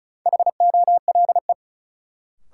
Morse code, heard as a keyed tone, about 36 wpm, 700 Hz. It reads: HOLE
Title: Sentences from Top 1600 Words - New Word Drill - 35wpm